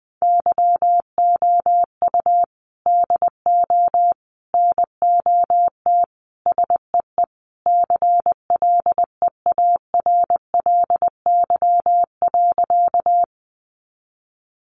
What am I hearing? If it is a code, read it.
Morse code, heard as a keyed tone, about 19 wpm, 700 Hz. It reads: YOU DO NOT SEE CLEARLY.
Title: Sentences from Top 1300 Words 20wpm